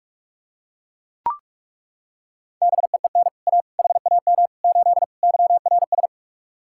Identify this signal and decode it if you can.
Morse code, heard as a keyed tone, about 38 wpm, 700 Hz. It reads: BEEN A HAM 8 YRS